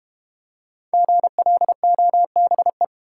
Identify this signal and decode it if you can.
Morse code, heard as a keyed tone, about 31 wpm, 700 Hz. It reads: GLOBE
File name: Easy Words - Speed-Racing with Hints - 25wpm